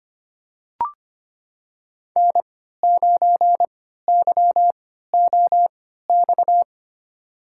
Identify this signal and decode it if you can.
Morse code, heard as a keyed tone, about 25 wpm, spaced as for 14 wpm, 700 Hz. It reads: N9YOX
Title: USA Call Signs - 5 Characters - 3x Character Spacing - 25wpm